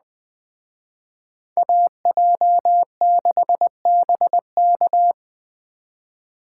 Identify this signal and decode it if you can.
Morse code, heard as a keyed tone, about 20 wpm, 700 Hz. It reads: AJ6BK